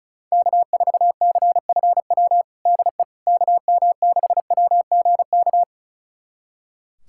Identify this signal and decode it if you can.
Morse code, heard as a keyed tone, about 36 wpm, 700 Hz. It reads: K4CFW DE KM6WGK
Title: USA Callsign DE Callsign 35wpm